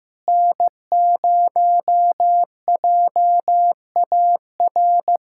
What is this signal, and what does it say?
Morse code, heard as a keyed tone, about 15 wpm, 700 Hz. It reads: N0JAR
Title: USA Callsign DE Callsign - 5 Characters - 15wpm